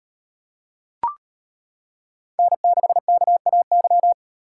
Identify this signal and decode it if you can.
Morse code, heard as a keyed tone, about 38 wpm, 700 Hz. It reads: N6KAY